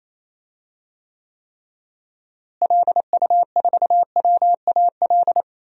Morse code, heard as a keyed tone, about 28 wpm, 700 Hz. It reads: LU4WAL
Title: DXCC-100 Call Signs - 6 Characters - Repeated 3x - 28wpm